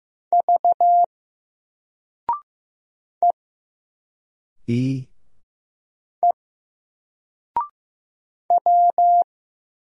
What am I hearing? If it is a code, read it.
Morse code, heard as a keyed tone, about 15 wpm, 700 Hz. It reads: V E E W
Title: Single Character (Letter:Number:Punctuation:Prosign) 15wpm